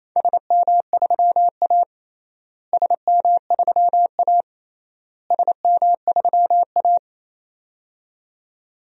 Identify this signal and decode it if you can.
Morse code, heard as a keyed tone, about 28 wpm, 700 Hz. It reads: SM3A SM3A SM3A
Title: DXCC-100 Call Signs - 4 Characters - Repeated 3x - 28wpm